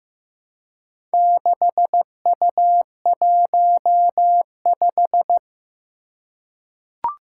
Morse code, heard as a keyed tone, about 15 wpm, 700 Hz. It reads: 6U15
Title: Four Challenging Letter-Number Combinations 15wpm